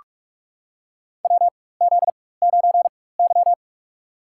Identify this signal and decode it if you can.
Morse code, heard as a keyed tone, about 42 wpm, spaced as for 20 wpm, 700 Hz. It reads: WZ9Y
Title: USA Call Signs - 4 Characters - 4x Character Spacing - 45wpm